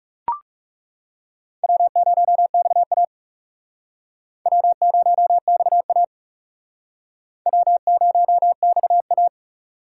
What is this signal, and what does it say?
Morse code, heard as a keyed tone, about 39 wpm, 700 Hz. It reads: W0XA W0XA W0XA